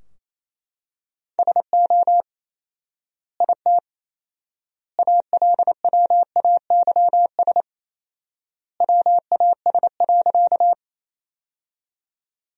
Morse code, heard as a keyed tone, about 27 wpm, 700 Hz. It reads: SO IT ALWAYS WAS.
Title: 4x Word-Spacing - Sentences from Top 300 Words - 28wpm